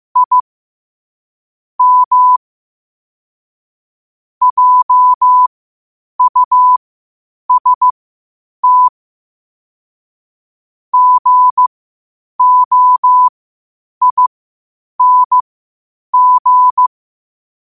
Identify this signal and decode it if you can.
Morse code, heard as a keyed tone, about 15 wpm, spaced as for 8 wpm, 1000 Hz. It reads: I M JUST GOING
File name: qotdfiles / 8wpm